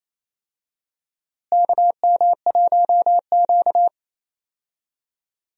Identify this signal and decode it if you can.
Morse code, heard as a keyed tone, about 28 wpm, 700 Hz. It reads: KM1Q
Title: USA Call Signs - 4 Characters - Repeated 3x - 28wpm